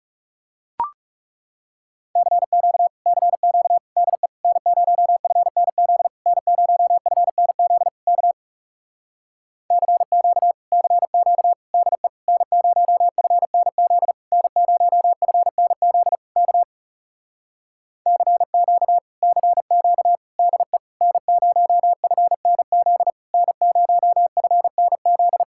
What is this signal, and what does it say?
Morse code, heard as a keyed tone, about 40 wpm, 700 Hz. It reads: CQ CQ DE N0FNZ N0FNZ K CQ CQ DE N0FNZ N0FNZ K CQ CQ DE N0FNZ N0FNZ